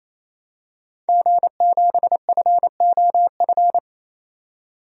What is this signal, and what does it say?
Morse code, heard as a keyed tone, about 28 wpm, 700 Hz. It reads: G7FOF
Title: DXCC-100 Call Signs - 5 Characters - 28wpm